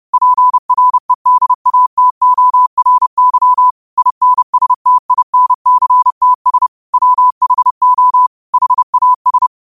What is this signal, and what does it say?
Morse code, heard as a keyed tone, about 30 wpm, 1000 Hz. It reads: PREDATORY INSTINCTS WHO HAS
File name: qotdfiles / 30wpm